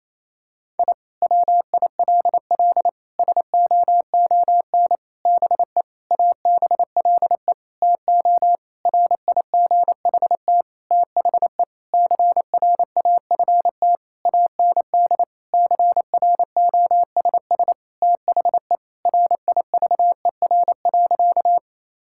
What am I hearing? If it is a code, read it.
Morse code, heard as a keyed tone, about 28 wpm, 700 Hz. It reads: I WILL SOON BE ABLE TO RIGHT THE CRAFT AND CROSS THE RIVER.
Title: Sentences from Top 1000 Words - Review Part 6 - 28wpm